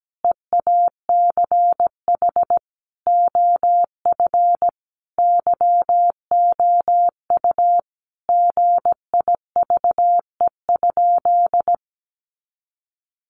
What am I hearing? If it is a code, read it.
Morse code, heard as a keyed tone, about 17 wpm, 700 Hz. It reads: EACH OF YOU GIVE?
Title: Sentences from Top 200 Words 17wpm